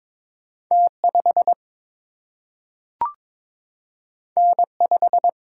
Morse code, heard as a keyed tone, about 22 wpm, 700 Hz. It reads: T 5 N 5
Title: Single-Letter Single-Number 22wpm